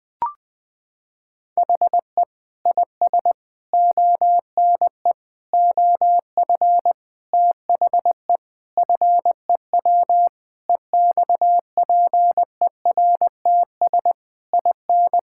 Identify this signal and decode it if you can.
Morse code, heard as a keyed tone, about 20 wpm, 700 Hz. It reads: HE IS ONE OF THE FEW EXPERTS IN
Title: Sentences with 4-Word Phrases - 20wpm